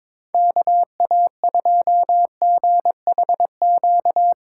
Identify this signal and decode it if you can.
Morse code, heard as a keyed tone, about 24 wpm, 700 Hz. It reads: KA2GHQ
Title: Speed Racing - USA 2x3 Call Signs 22wpm